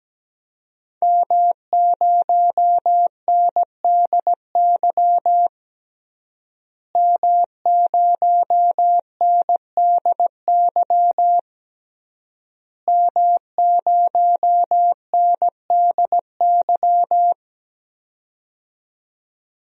Morse code, heard as a keyed tone, about 17 wpm, 700 Hz. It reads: M0NDY M0NDY M0NDY
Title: DXCC-100 Call Signs - 5 Characters - Repeated 3x - 17wpm